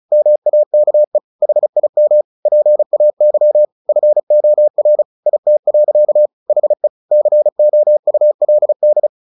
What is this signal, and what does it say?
Morse code, heard as a keyed tone, about 37 wpm, 600 Hz. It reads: MAKE HIM PAY FOR IT. HE COULD
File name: the old man and the sea 2 35wpm